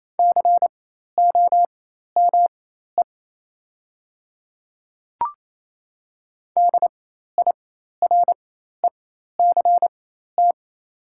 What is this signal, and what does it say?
Morse code, heard as a keyed tone, about 28 wpm, spaced as for 13 wpm, 700 Hz. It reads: COME DIRECT
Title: 4x Character Spacing - Top 500 Words - 28wpm